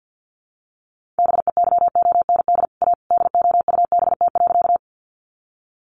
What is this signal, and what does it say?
Morse code, heard as a keyed tone, about 50 wpm, 700 Hz. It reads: BEYOND A DOUBT.
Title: Sentences from Top 1500 Words 50wpm